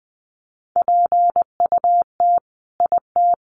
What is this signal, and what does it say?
Morse code, heard as a keyed tone, about 20 wpm, 700 Hz. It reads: PUT IT
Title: Sentences from Top 1400 Words 20wpm